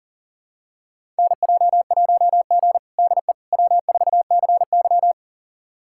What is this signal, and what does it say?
Morse code, heard as a keyed tone, about 37 wpm, 700 Hz. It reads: NJ1G DE W4CY
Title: USA Callsign DE Callsign - 4 Characters - 40wpm